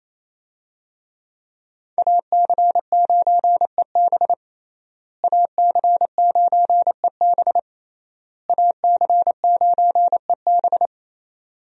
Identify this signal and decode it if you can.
Morse code, heard as a keyed tone, about 28 wpm, 700 Hz. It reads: AC9EB AC9EB AC9EB
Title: USA Call Signs - 5 Characters - Repeated 3x - 28wpm